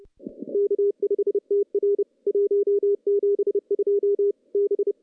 Morse code, heard as a keyed tone, about 29 wpm, 410 Hz. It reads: K5TR 172 B
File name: contests / CW-SS-k5tr- -0200